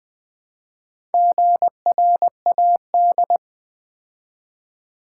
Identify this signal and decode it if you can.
Morse code, heard as a keyed tone, about 20 wpm, 700 Hz. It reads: GRAD
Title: Build A Word - 2 Letters At A Time - 8-Letter Words - 20wpm